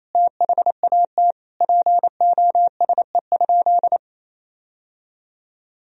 Morse code, heard as a keyed tone, about 28 wpm, 700 Hz. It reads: THAT POSE?
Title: Sentences from Top 1000 Words - Review Part 2 - 28wpm